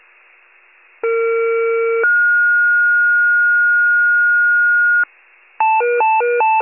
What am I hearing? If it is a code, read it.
Two-tone page, heard as a one-second tone, then a three-second tone, about 460 then 1430 Hz. It is heard from 1.0 to 5.0 s.